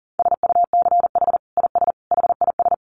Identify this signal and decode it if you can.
Morse code, heard as a keyed tone, about 40 wpm, 700 Hz. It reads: SUCH IS HIS
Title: Sentences from Top 900 Words 40wpm